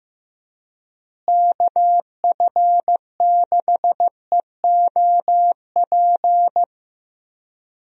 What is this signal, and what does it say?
Morse code, heard as a keyed tone, about 16 wpm, 700 Hz. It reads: KF6EOP
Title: USA Call Signs - 6 Characters - Repeated 3x - 15wpm